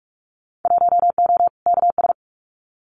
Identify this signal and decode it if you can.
Morse code, heard as a keyed tone, about 46 wpm, 700 Hz. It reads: 1O KS
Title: Winter Field Day Exchanges 45wpm